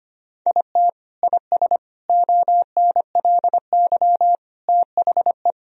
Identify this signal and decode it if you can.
Morse code, heard as a keyed tone, about 25 wpm, 700 Hz. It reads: IT IS ONLY THE
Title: Sentences from Top 1000 Words - Review Part 2 - 25wpm